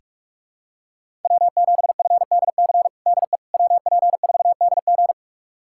Morse code, heard as a keyed tone, about 47 wpm, 700 Hz. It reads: W7FDC DE WP4DG
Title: USA Callsign DE Callsign - 5 Characters - 45wpm